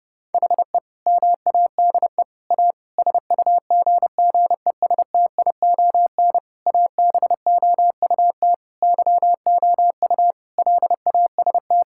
Morse code, heard as a keyed tone, about 30 wpm, 700 Hz. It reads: HE MADE A SUGGESTION ABOUT YOU LAST
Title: Sentences from Top 1600 Words 30wpm